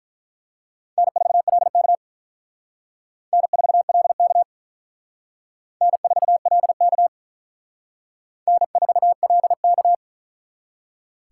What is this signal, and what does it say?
Morse code, heard as a keyed tone, about 45 wpm, 700 Hz. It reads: N4LK N4LK N4LK N4LK